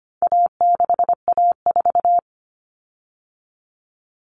Morse code, heard as a keyed tone, about 25 wpm, 700 Hz. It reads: A6A4
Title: Common Cut Numbers - 4-Digit 25wpm